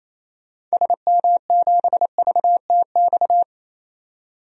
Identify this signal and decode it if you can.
Morse code, heard as a keyed tone, about 27 wpm, 700 Hz. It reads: SM7VTX
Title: DXCC-100 Call Signs - 6 Characters - 28wpm